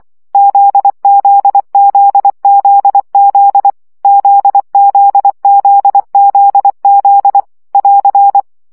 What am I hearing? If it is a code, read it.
Morse code, heard as a keyed tone, about 24 wpm, 800 Hz. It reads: ZZZZZ ZZZZZ <AR>
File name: CW gyakorlas z 20 baud